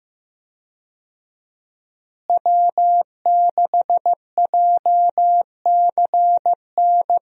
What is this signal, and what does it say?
Morse code, heard as a keyed tone, about 16 wpm, 700 Hz. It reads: W6JCN
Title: USA Call Signs - 5 Characters - Repeated 2x - 15wpm